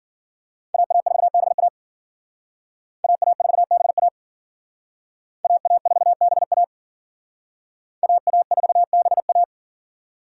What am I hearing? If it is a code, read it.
Morse code, heard as a keyed tone, about 52 wpm, 700 Hz. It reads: AA4BA AA4BA AA4BA AA4BA